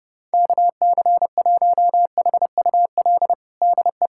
Morse code, heard as a keyed tone, about 32 wpm, 700 Hz. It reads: KC1HUL DE
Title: USA Callsign DE Callsign - 6 Characters - 30wpm